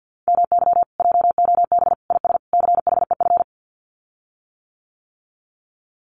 Morse code, heard as a keyed tone, about 50 wpm, 700 Hz. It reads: MY JOB IS CHEF
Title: Common QSO Phrases Encore 50wpm